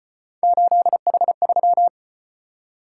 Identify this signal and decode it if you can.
Morse code, heard as a keyed tone, about 36 wpm, 700 Hz. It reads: 8H3